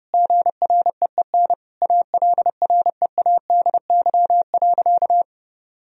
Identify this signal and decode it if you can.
Morse code, heard as a keyed tone, about 30 wpm, 700 Hz. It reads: GREEN ALREADY.
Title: Sentences from Top 1000 Words - Review Part 2 - 30wpm